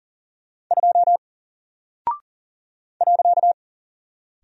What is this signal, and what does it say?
Morse code, heard as a keyed tone, about 40 wpm, 700 Hz. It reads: 2 .
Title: Single Character (Letter:Number:Punctuation:Prosign) 40wpm